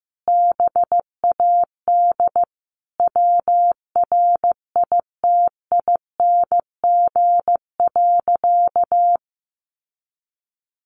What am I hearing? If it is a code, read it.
Morse code, heard as a keyed tone, about 15 wpm, 700 Hz. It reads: BAD WRITING.
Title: Sentences from Top 800 Words 15wpm